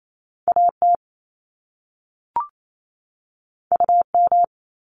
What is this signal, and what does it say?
Morse code, heard as a keyed tone, about 28 wpm, 700 Hz. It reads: AT UM
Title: Top Two Letter Combinations in Words 28wpm